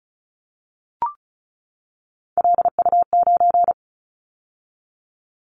Morse code, heard as a keyed tone, about 34 wpm, 700 Hz. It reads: LU9
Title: Callsign Prefixes for Top 100 DXCC - 35wpm